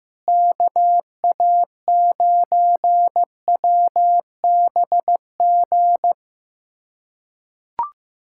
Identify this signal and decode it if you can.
Morse code, heard as a keyed tone, about 15 wpm, 700 Hz. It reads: KA9WBG
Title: USA Callsign DE Callsign - 6 Characters - 15wpm